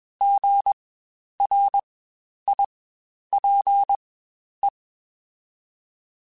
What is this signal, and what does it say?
Morse code, heard as a keyed tone, about 21 wpm, spaced as for 10 wpm, 800 Hz. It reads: GRIPE
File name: Spaced Alphas / Spaced Alphas mp3s / Gs sp @20